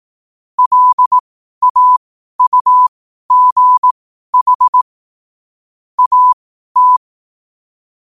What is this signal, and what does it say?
Morse code, heard as a keyed tone, about 18 wpm, spaced as for 12 wpm, 1000 Hz. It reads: LAUGH AT
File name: qotdfiles / 12wpm